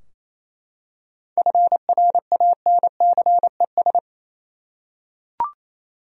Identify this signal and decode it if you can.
Morse code, heard as a keyed tone, about 28 wpm, 700 Hz. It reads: FRANCES
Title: USA Common Names - 28wpm